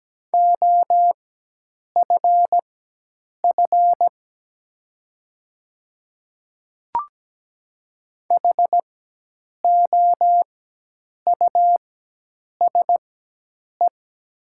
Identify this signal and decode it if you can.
Morse code, heard as a keyed tone, about 17 wpm, spaced as for 8 wpm, 700 Hz. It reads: OFF HOUSE
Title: 4x Character Spacing - Top 200 Words - 17wpm